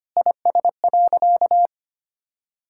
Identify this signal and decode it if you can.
Morse code, heard as a keyed tone, about 25 wpm, 700 Hz. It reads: IS.
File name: practice / 25wpm / Sentences with 4-Word Phrases - 25wpm